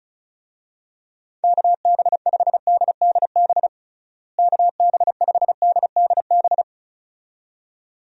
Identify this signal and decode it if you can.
Morse code, heard as a keyed tone, about 34 wpm, 700 Hz. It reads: KB5DDB KB5DDB
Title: USA Call Signs - 6 Characters - Repeated 2x - 35wpm